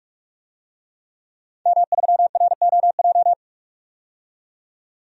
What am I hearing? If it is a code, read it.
Morse code, heard as a keyed tone, about 46 wpm, 700 Hz. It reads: M3ROJ
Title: DXCC-100 Call Signs - 5 Characters - 45wpm